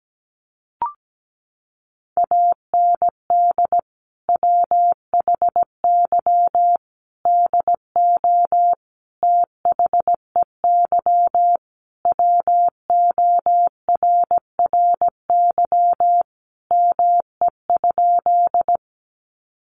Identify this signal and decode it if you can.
Morse code, heard as a keyed tone, about 17 wpm, 700 Hz. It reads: AND WHY DO THEY WORRY ME?
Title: Sentences from Top 1100 Words 17wpm